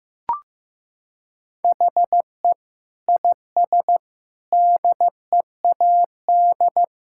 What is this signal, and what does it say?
Morse code, heard as a keyed tone, about 15 wpm, 700 Hz. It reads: HE IS DEAD
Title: Sentences from Top 1600 Words 15wpm